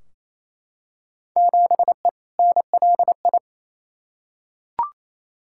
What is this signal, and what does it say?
Morse code, heard as a keyed tone, about 28 wpm, 700 Hz. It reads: 7E NLI
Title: Field Day Exchanges Encore - 28wpm